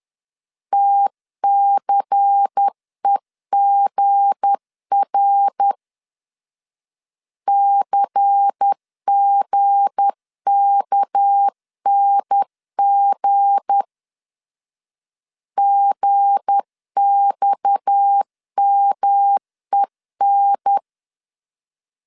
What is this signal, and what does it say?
Morse code, heard as a keyed tone, about 10 wpm, 800 Hz. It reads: TCEGR CGKNG GXMEN
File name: LEZIONI / 7 GF9Z / 7 G 10wpm 2501